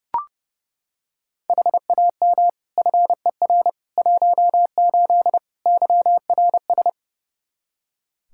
Morse code, heard as a keyed tone, about 30 wpm, 700 Hz. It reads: HAM FER 18 YRS
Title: Practice QSO - Exchange 3 - More Info - 30wpm